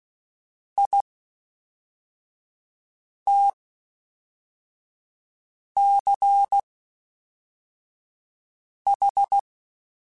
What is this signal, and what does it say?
Morse code, heard as a keyed tone, about 16 wpm, 800 Hz. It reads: I T C H
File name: Special INGs / Special INGs-mp3s / Fast ing @10